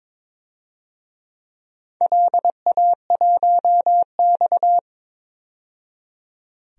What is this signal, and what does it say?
Morse code, heard as a keyed tone, about 22 wpm, 700 Hz. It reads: LA1X
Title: DXCC-100 Call Signs - 4 Characters - 22wpm